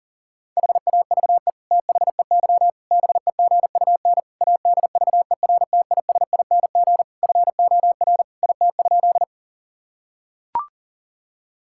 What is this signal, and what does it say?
Morse code, heard as a keyed tone, about 40 wpm, 700 Hz. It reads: HAVE THEY BEGUN ADVERTISING FOR IT?
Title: Sentences from Top 1000 Words - Review Part 6 - 40wpm